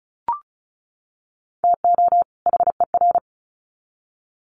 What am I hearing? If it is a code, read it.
Morse code, heard as a keyed tone, about 34 wpm, 700 Hz. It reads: TO HER
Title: Sets of 2 Words 35wpm